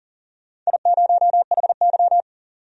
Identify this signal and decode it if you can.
Morse code, heard as a keyed tone, about 40 wpm, 700 Hz. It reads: I0HY
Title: DXCC-100 Call Signs - 4 Characters - 40wpm